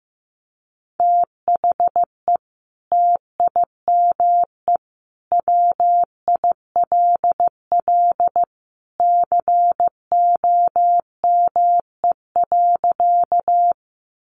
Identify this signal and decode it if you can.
Morse code, heard as a keyed tone, about 15 wpm, 700 Hz. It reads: THE TIME WILL COME.
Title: Sentences from Top 100 Words 15wpm